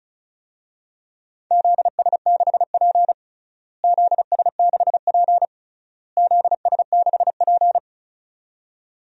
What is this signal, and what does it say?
Morse code, heard as a keyed tone, about 34 wpm, 700 Hz. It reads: ZS6P ZS6P ZS6P
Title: DXCC-100 Call Signs - 4 Characters - Repeated 3x - 35wpm